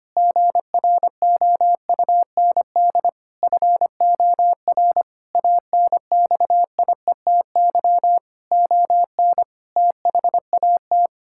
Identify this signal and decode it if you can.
Morse code, heard as a keyed tone, about 25 wpm, 700 Hz. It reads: GROUND FOR ANXIETY ON THAT
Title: Sentences from Top 1600 Words 25wpm